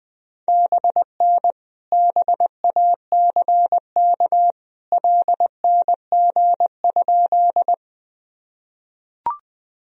Morse code, heard as a keyed tone, about 20 wpm, 700 Hz. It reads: BN BACK LNG?
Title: Sentences from Top 200 Words and Abbreviations 20wpm